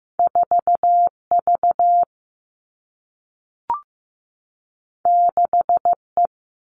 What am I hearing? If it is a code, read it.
Morse code, heard as a keyed tone, about 15 wpm, 700 Hz. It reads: 4 V 6 E
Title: Two Challenging Letter-Number Combinations 15wpm